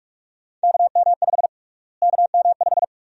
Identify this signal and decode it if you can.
Morse code, heard as a keyed tone, about 46 wpm, 700 Hz. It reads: KM5 KM5
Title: USA Call Sign Prefixes - Repeated 2x - 45wpm